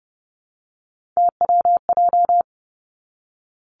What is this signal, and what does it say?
Morse code, heard as a keyed tone, about 30 wpm, 700 Hz. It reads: TWJ
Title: Three Challenging Letter-Number Combinations 30wpm